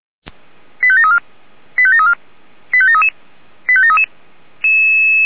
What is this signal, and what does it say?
Five-tone selective call (ZVEI), heard as quick runs of five tones, tones of about 70 ms. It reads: 85734 85734 85733 85733